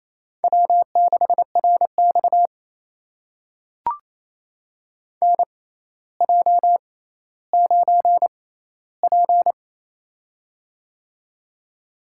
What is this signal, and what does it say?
Morse code, heard as a keyed tone, about 28 wpm, spaced as for 14 wpm, 700 Hz. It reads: W6RX NJ9P
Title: USA Call Signs - 4 Characters - 6x Character Spacing - 28wpm